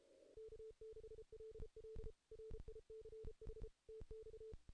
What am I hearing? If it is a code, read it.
Morse code, heard as a keyed tone, about 31 wpm, 440 Hz. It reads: K6LL RICH TX
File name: 2016 JAN CW NAQP / 2016 Jan naqp k5ot - 0-6